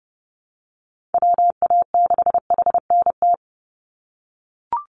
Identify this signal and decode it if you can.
Morse code, heard as a keyed tone, about 30 wpm, 700 Hz. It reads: WA6HNT
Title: USA Call Signs 30wpm